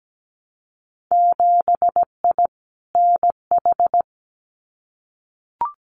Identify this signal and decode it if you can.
Morse code, heard as a keyed tone, about 17 wpm, 700 Hz. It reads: 7I NH
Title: Winter Field Day Exchanges Encore 17wpm